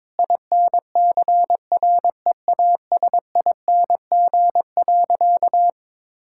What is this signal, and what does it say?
Morse code, heard as a keyed tone, about 22 wpm, 700 Hz. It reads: INCREASING.
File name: practice / 22wpm / Sentences with 2-Word Phrases - 22wpm